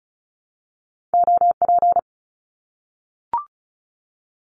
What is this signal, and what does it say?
Morse code, heard as a keyed tone, about 36 wpm, 700 Hz. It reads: OP
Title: Abbreviations 35wpm